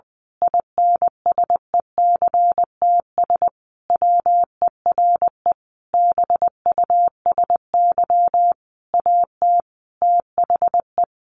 Morse code, heard as a keyed tone, about 20 wpm, 700 Hz. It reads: INSECTS WERE BUSY AT THE
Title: A Sound Of Thunder 20wpm